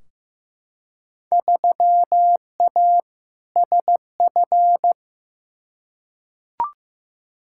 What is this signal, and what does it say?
Morse code, heard as a keyed tone, about 15 wpm, 700 Hz. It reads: 3A SF
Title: Field Day Exchanges Encore - 15wpm